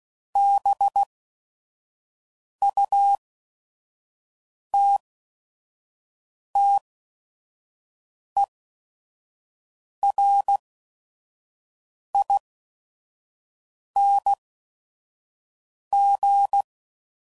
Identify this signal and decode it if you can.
Morse code, heard as a keyed tone, about 16 wpm, spaced as for 5 wpm, 800 Hz. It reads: BUTTERING
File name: Suffixes / Special ING's / VerySlowToFaster ING's